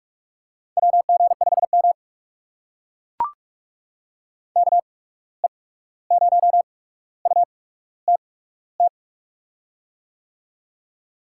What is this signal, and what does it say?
Morse code, heard as a keyed tone, about 44 wpm, spaced as for 17 wpm, 700 Hz. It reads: WG5M KE0UTT
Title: USA Call Signs - 8x Character Spacing - 45wpm